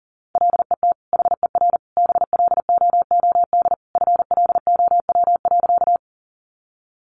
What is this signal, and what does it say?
Morse code, heard as a keyed tone, about 40 wpm, 700 Hz. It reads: LET HER BLOOD FLOW.
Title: Sentences from Top 1200 Words 40wpm